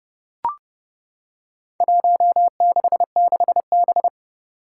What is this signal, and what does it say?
Morse code, heard as a keyed tone, about 31 wpm, 700 Hz. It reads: 166B
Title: Four Challenging Letter-Number Combinations 30wpm